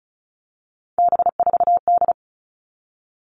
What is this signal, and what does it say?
Morse code, heard as a keyed tone, about 34 wpm, 700 Hz. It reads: B4D
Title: All Cut Numbers - 3-Digit 35wpm